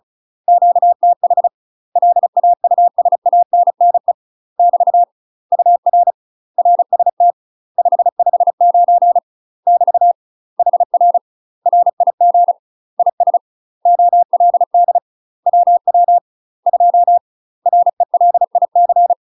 Morse code, heard as a keyed tone, about 33 wpm, 700 Hz. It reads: QTH LAUSANNE <BT> UR RST 559 <BT> HR RIG IS OLD WW 2 RELIC
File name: UK MASTER QSOs TRAINING / 35WPM / UK MASTER QSOs 0035